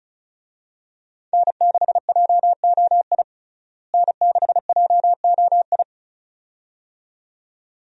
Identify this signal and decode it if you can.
Morse code, heard as a keyed tone, about 36 wpm, 700 Hz. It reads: N6JOI N6JOI
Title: USA Call Signs - 5 Characters - Repeated 2x - 35wpm